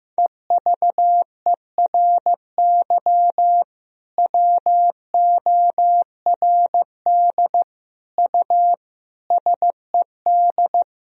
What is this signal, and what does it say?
Morse code, heard as a keyed tone, about 15 wpm, 700 Hz. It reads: EVERY WORD U SED
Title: Sentences from Top 400 Words and Abbreviations 15wpm